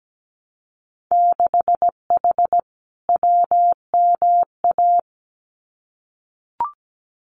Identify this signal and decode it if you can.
Morse code, heard as a keyed tone, about 17 wpm, 700 Hz. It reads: 6H WMA
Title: Winter Field Day Exchanges Encore 17wpm